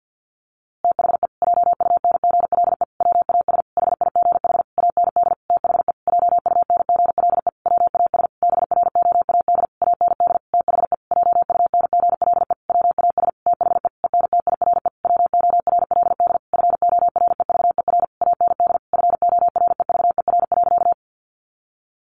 Morse code, heard as a keyed tone, about 48 wpm, 700 Hz. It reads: THE JUNGLE WAS HIGH AND THE JUNGLE WAS BROAD AND THE JUNGLE WAS THE ENTIRE WORLD FOREVER AND FOREVER.
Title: A Sound Of Thunder 50wpm